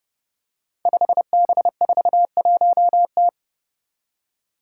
Morse code, heard as a keyed tone, about 30 wpm, 700 Hz. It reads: 5B41T
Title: Five Challenging Letter-Number Combinations 30wpm